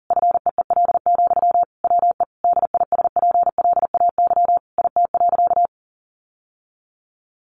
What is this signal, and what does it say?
Morse code, heard as a keyed tone, about 37 wpm, 700 Hz. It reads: FEEL, WE DISPLAY IT.
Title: Sentences from Top 1200 Words 40wpm